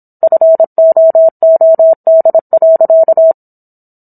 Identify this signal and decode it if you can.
Morse code, heard as a keyed tone, about 27 wpm, 650 Hz. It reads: FOOD.